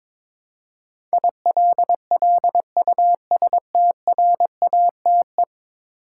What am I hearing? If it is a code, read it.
Morse code, heard as a keyed tone, about 22 wpm, 700 Hz. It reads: ILLUSTRATE
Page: morse-code-ninja-practice-22wpm